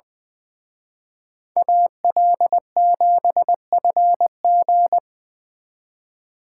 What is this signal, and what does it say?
Morse code, heard as a keyed tone, about 24 wpm, 700 Hz. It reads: AL7FG
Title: Speed Racing - USA 2x2 Call Signs 20wpm